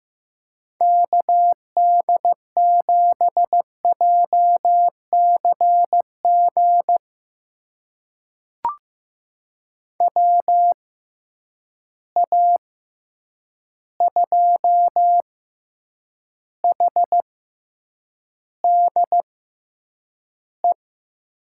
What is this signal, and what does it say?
Morse code, heard as a keyed tone, about 15 wpm, spaced as for 8 wpm, 700 Hz. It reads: KD7JCG WA2HDE
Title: USA Call Signs - 6 Characters - 6x Character Spacing - 15wpm